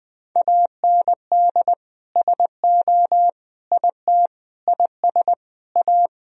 Morse code, heard as a keyed tone, about 20 wpm, 700 Hz. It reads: AND SO IT IS A
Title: Sentences from Top 1000 Words - Review Part 3 - 20wpm